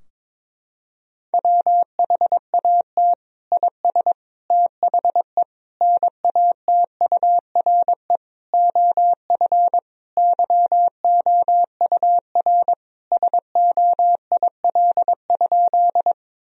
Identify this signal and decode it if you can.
Morse code, heard as a keyed tone, about 22 wpm, 700 Hz. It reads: WHAT IS THE NATURE OF YOUR SOIL?
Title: Sentences from Top 1000 Words - Review Part 4 - 22wpm